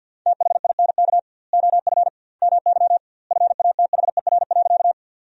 Morse code, heard as a keyed tone, about 50 wpm, 700 Hz. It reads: THINK OF MY FATHER.
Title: 2x Word-Spacing - Sentences from Top 300 Words - 50wpm